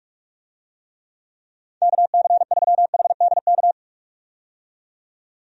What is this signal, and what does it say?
Morse code, heard as a keyed tone, about 41 wpm, 700 Hz. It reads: KC3HDK
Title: USA Call Signs - 6 Characters - 45wpm